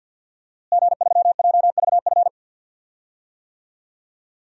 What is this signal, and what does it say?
Morse code, heard as a keyed tone, about 48 wpm, 700 Hz. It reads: G3JVF
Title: DXCC-100 Call Signs - 5 Characters - Repeated 2x - 50wpm